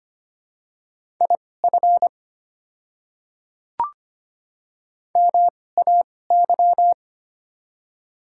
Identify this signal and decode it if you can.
Morse code, heard as a keyed tone, about 25 wpm, spaced as for 18 wpm, 700 Hz. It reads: IF MAY
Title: 2x Character Spacing - Top 100 Words - 25wpm